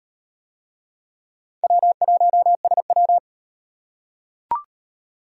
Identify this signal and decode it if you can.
Morse code, heard as a keyed tone, about 38 wpm, 700 Hz. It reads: W1SW